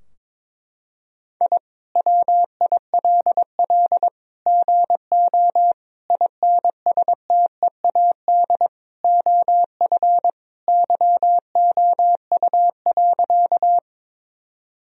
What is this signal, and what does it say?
Morse code, heard as a keyed tone, about 22 wpm, 700 Hz. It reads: I WILL GO INSTEAD OF YOU.
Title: Sentences from Top 1000 Words - Review Part 4 - 22wpm